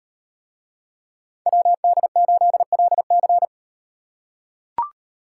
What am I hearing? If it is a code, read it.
Morse code, heard as a keyed tone, about 38 wpm, 700 Hz. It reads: WD8LC